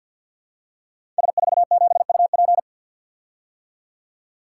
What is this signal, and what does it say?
Morse code, heard as a keyed tone, about 53 wpm, 700 Hz. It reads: S47UP
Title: Code Groups - 5 Letter:Numbers - 50wpm